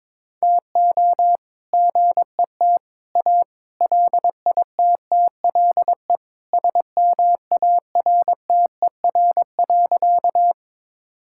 Morse code, heard as a keyed tone, about 22 wpm, 700 Hz. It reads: TO GET A LITTLE SMARTER.
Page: morse-code-ninja-practice-22wpm